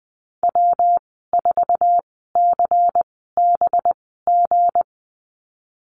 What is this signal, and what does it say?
Morse code, heard as a keyed tone, about 19 wpm, spaced as for 14 wpm, 700 Hz. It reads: W4CBG
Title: 219 Lesson 35g - Number 0 - Call Signs 20wpm